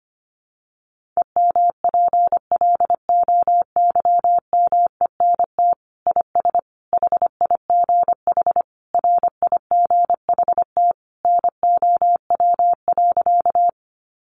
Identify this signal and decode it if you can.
Morse code, heard as a keyed tone, about 25 wpm, 700 Hz. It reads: EMPLOYMENT IS HIGH RIGHT NOW.
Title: Sentences from Top 1500 Words 25wpm